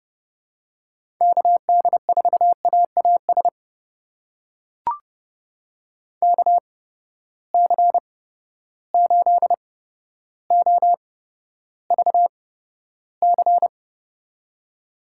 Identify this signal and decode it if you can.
Morse code, heard as a keyed tone, about 30 wpm, spaced as for 13 wpm, 700 Hz. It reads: KD4AAS KC8OVC
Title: USA Call Signs - 6 Characters - 8x Character Spacing - 30wpm